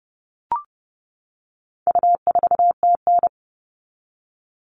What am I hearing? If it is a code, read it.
Morse code, heard as a keyed tone, about 30 wpm, 700 Hz. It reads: U4TN
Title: All Cut Numbers - 4-Digit 30wpm